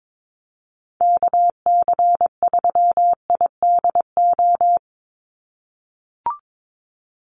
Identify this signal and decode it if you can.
Morse code, heard as a keyed tone, about 22 wpm, 700 Hz. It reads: KC3IDO
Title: USA Call Signs 22wpm